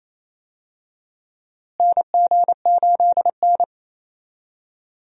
Morse code, heard as a keyed tone, about 28 wpm, 700 Hz. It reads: NG8N
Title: USA Call Signs - 4 Characters - Repeated 2x - 28wpm